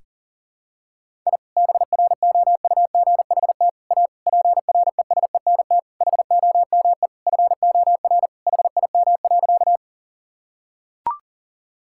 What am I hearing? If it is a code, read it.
Morse code, heard as a keyed tone, about 40 wpm, 700 Hz. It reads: I BROUGHT A PRESENT HOME FOR HIM.
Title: Sentences from Top 1000 Words - Review Part 5 - 40wpm